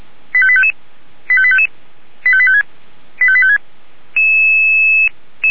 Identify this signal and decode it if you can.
Five-tone selective call (ZVEI), heard as quick runs of five tones, tones of about 70 ms. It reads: 85855 85855 85856 85856